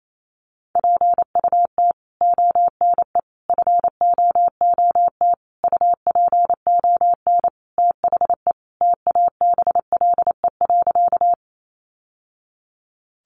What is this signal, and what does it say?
Morse code, heard as a keyed tone, about 28 wpm, 700 Hz. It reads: PUT ONE FOOT UPON THE TABLE.
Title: Sentences from Top 1100 Words 28wpm